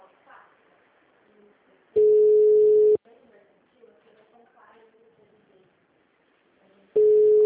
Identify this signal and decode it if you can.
Ringback tone (Europe, 425 Hz), with speech.